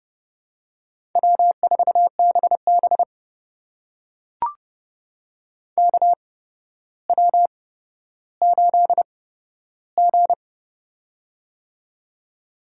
Morse code, heard as a keyed tone, about 31 wpm, spaced as for 13 wpm, 700 Hz. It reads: W4BB KW8G
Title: USA Call Signs - 4 Characters - 8x Character Spacing - 30wpm